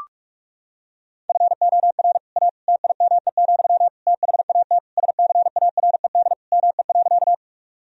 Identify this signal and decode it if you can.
Morse code, heard as a keyed tone, about 46 wpm, 700 Hz. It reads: FOR A TIME, THAT SCARED ME.
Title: Sentences from Top 1000 Words - Review Part 4 - 45wpm